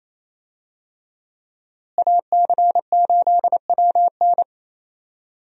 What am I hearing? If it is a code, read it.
Morse code, heard as a keyed tone, about 27 wpm, 700 Hz. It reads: AC8WN
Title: USA Call Signs - 5 Characters - Repeated 2x - 28wpm